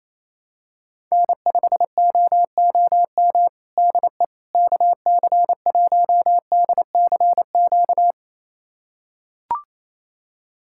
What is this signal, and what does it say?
Morse code, heard as a keyed tone, about 28 wpm, 700 Hz. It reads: N5OOM DE KC1DCQ
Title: USA Callsign DE Callsign 28wpm